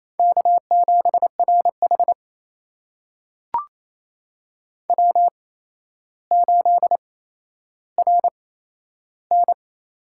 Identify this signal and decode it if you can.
Morse code, heard as a keyed tone, about 28 wpm, spaced as for 12 wpm, 700 Hz. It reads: K7RH W8RN
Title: USA Call Signs - 4 Characters - 8x Character Spacing - 28wpm